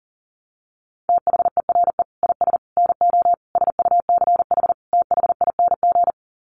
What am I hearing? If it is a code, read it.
Morse code, heard as a keyed tone, about 40 wpm, 700 Hz. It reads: THERE IS NO SUCH THING
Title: Sets of 5 Words Encore 40wpm